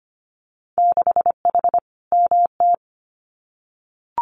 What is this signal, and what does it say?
Morse code, heard as a keyed tone, about 25 wpm, 700 Hz. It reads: 6H MT
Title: Winter Field Day Exchanges Encore 25wpm